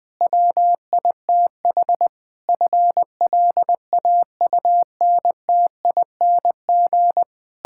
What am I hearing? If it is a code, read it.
Morse code, heard as a keyed tone, about 20 wpm, 700 Hz. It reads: WITH FLAUNTING
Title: Pangram Challenge v2 - 20wpm